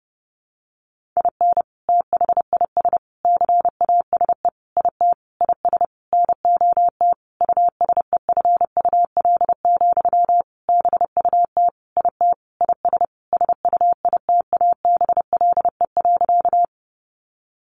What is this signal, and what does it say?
Morse code, heard as a keyed tone, about 30 wpm, 700 Hz. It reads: IN THIS CASE IT IS NOT USEFUL, BUT IT IS SUITABLE.
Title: Sentences from Top 1400 Words 30wpm